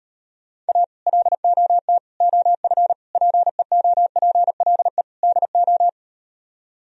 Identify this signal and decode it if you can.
Morse code, heard as a keyed tone, about 38 wpm, 700 Hz. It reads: A LOT OF PEOPLE DO